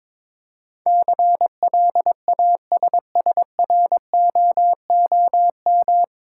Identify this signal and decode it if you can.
Morse code, heard as a keyed tone, about 22 wpm, 700 Hz. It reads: CLASSROOM
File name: 9 Letter Words - First 3 Letters Given - 22wpm